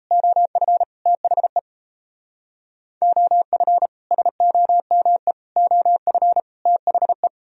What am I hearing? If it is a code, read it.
Morse code, heard as a keyed tone, about 36 wpm, 700 Hz. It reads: OF THE OF SOME OF THE